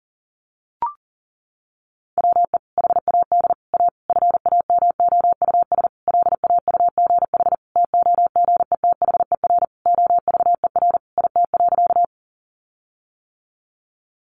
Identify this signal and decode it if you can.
Morse code, heard as a keyed tone, about 40 wpm, 700 Hz. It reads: WE HAD A FAMOUS LAUGH TOGETHER OVER IT.
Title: Sentences from Top 1500 Words 40wpm